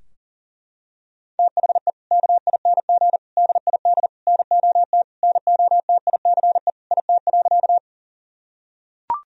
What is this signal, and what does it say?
Morse code, heard as a keyed tone, about 40 wpm, 700 Hz. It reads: THE KING DID NOT NOTICE IT.
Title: Sentences from Top 1000 Words - Review Part 2 - 40wpm